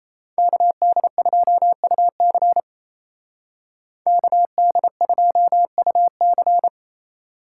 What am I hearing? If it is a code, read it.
Morse code, heard as a keyed tone, about 35 wpm, 700 Hz. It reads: KD2UC KD2UC
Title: Speed Racing - USA Call Signs 28wpm